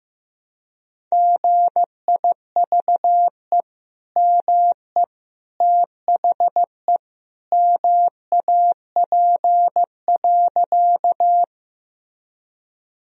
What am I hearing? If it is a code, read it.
Morse code, heard as a keyed tone, about 15 wpm, 700 Hz. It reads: GIVE ME THE MAP.
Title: Sentences from Top 1000 Words - Review Part 2 - 15wpm